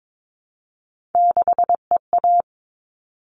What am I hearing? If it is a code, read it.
Morse code, heard as a keyed tone, about 22 wpm, 700 Hz. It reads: 6EA
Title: All Cut Numbers - 3-Digit 22wpm